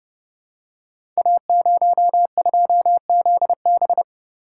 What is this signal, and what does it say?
Morse code, heard as a keyed tone, about 30 wpm, 690 Hz. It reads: A02ZB
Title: Code Groups - 5 Letter:Numbers - 30wpm